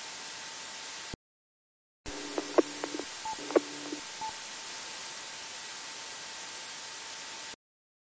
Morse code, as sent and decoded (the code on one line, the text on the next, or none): none